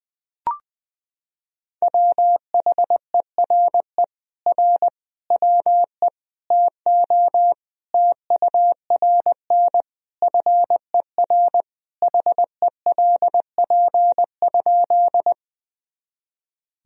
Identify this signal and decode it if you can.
Morse code, heard as a keyed tone, about 20 wpm, 700 Hz. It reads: WHERE R WE TO TURN FER HELP?
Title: Sentences from Top 200 Words and Abbreviations 20wpm